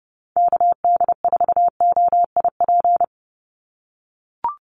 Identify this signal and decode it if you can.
Morse code, heard as a keyed tone, about 30 wpm, 700 Hz. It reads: KD4OIP
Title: USA Call Signs 30wpm